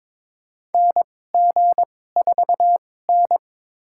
Morse code, heard as a keyed tone, about 22 wpm, spaced as for 16 wpm, 700 Hz. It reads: NG4N
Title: USA Call Signs - 2x Character Spacing - 22wpm